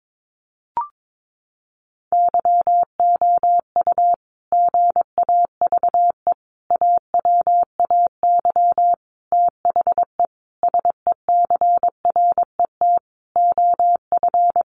Morse code, heard as a keyed tone, about 22 wpm, 700 Hz. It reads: YOU GAVE AWAY THE SECRET OF
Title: Sentences from Top 1300 Words 22wpm